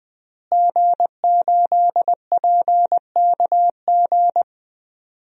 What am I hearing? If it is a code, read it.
Morse code, heard as a keyed tone, about 20 wpm, 700 Hz. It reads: G8PKG
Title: DXCC-100 Call Signs - 5 Characters - Repeated 3x - 20wpm